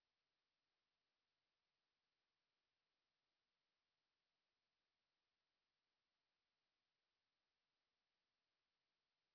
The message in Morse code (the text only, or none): none